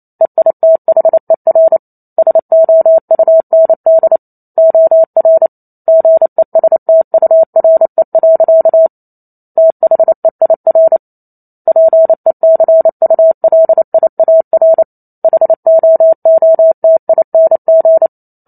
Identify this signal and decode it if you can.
Morse code, heard as a keyed tone, about 28 wpm, 650 Hz. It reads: EITHER SOUND OR GESTURE. THEIR PECULIAR HOOTING
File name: Chapter 2 - 28 WPM